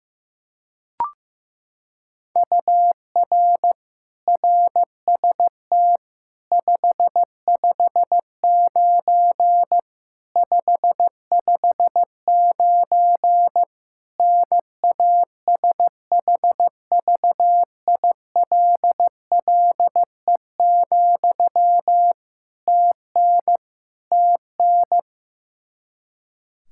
Morse code, heard as a keyed tone, about 15 wpm, 700 Hz. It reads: UR RST 559 559 NASHVILLE, TN TN
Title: Practice QSO - Exchange 1 - Beginning - 15wpm